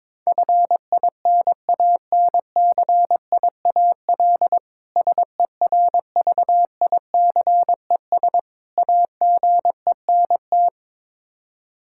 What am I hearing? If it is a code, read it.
Morse code, heard as a keyed tone, about 22 wpm, 700 Hz. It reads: FINANCIAL SERVICES AGENT
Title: Practice QSO - Exchange 3 - More Info - 22wpm